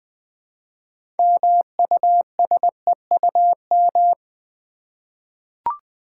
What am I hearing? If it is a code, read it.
Morse code, heard as a keyed tone, about 20 wpm, 700 Hz. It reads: MUSEUM
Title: 6 Letter Words - First 3 Letters Given - 20wpm